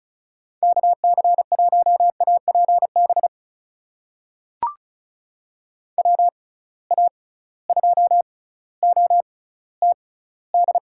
Morse code, heard as a keyed tone, about 36 wpm, spaced as for 18 wpm, 700 Hz. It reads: KC1APB WA2OTD
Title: USA Call Signs - 6 Characters - 6x Character Spacing - 35wpm